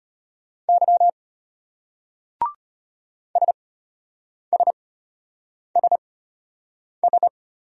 Morse code, heard as a keyed tone, about 34 wpm, spaced as for 8 wpm, 700 Hz. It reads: Y SSSS